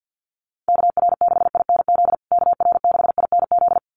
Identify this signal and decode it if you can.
Morse code, heard as a keyed tone, about 48 wpm, 700 Hz. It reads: KR6INZ KR6INZ
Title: USA Call Signs Repeated 2x 50wpm